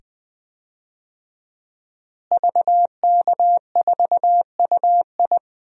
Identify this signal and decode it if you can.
Morse code, heard as a keyed tone, about 20 wpm, 700 Hz. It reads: VK4UI
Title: DXCC-100 Call Signs - 5 Characters - Repeated 2x - 20wpm